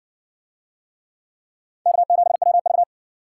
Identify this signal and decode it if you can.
Morse code, heard as a keyed tone, about 57 wpm, 700 Hz. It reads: K7WV